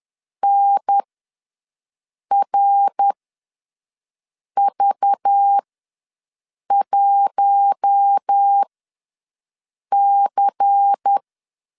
Morse code, heard as a keyed tone, about 10 wpm, spaced as for 5 wpm, 800 Hz. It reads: NRV1C
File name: LEZIONI / 6 PKX0 / 6 PKX0 ws 10wpm 4220